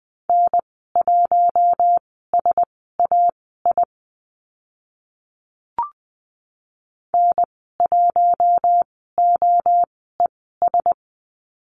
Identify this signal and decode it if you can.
Morse code, heard as a keyed tone, about 20 wpm, spaced as for 15 wpm, 700 Hz. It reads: N1SAI N1OES
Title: 048 Lesson 9g - Number 4 - Call Signs 20wpm